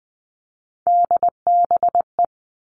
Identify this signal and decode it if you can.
Morse code, heard as a keyed tone, about 20 wpm, 700 Hz. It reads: DBE
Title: All Cut Numbers - 3-Digit 20wpm